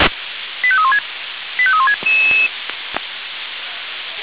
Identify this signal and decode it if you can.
Five-tone selective call (ZVEI), heard as quick runs of five tones, tones of about 70 ms. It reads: 85316 85316
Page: SDR/ZVEI